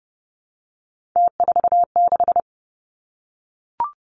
Morse code, heard as a keyed tone, about 30 wpm, 700 Hz. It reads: T46
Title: Three Challenging Letter-Number Combinations 30wpm